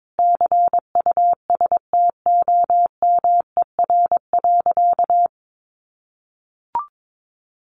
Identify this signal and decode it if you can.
Morse code, heard as a keyed tone, about 22 wpm, 700 Hz. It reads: CUSTOMER.
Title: Sentences from Top 1100 Words 22wpm